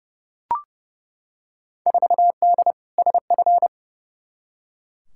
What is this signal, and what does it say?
Morse code, heard as a keyed tone, about 30 wpm, 700 Hz. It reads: 4D SF
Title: Field Day Exchanges Encore - 30wpm